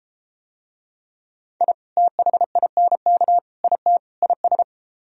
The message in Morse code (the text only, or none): I THINK IT IS